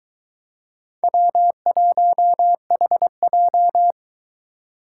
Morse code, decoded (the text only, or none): W1HJ